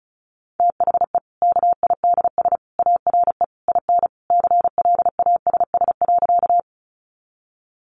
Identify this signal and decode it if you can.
Morse code, heard as a keyed tone, about 34 wpm, 700 Hz. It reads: THE KIDS ARE IN CLASS.
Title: Easy Phrases 35wpm